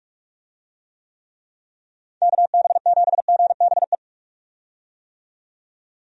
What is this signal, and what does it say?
Morse code, heard as a keyed tone, about 46 wpm, 700 Hz. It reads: KB7GBE
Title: USA Call Signs - 6 Characters - 45wpm